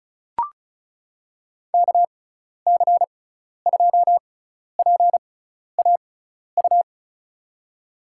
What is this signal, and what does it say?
Morse code, heard as a keyed tone, about 36 wpm, spaced as for 12 wpm, 700 Hz. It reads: KC2PAU
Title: USA Call Signs - 6 Characters - 6x Character Spacing - 35wpm